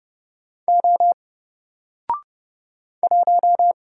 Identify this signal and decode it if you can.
Morse code, heard as a keyed tone, about 30 wpm, 700 Hz. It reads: O 1
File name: Single-Letter-Number-30wpm